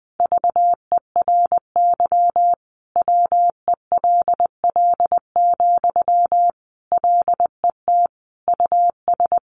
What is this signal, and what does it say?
Morse code, heard as a keyed tone, about 20 wpm, 700 Hz. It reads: VERY WELL, LET US
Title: Sentences from Top 1200 Words 20wpm